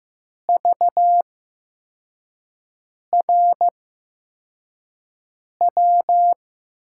Morse code, heard as a keyed tone, about 15 wpm, spaced as for 4 wpm, 700 Hz. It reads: VRW
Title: USA Call Signs - 6 Characters - 8x Character Spacing - 15wpm